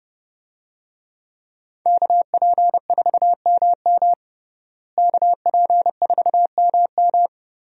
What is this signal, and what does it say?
Morse code, heard as a keyed tone, about 30 wpm, 700 Hz. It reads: KP4MM KP4MM
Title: USA Call Signs - 5 Characters - Repeated 2x - 30wpm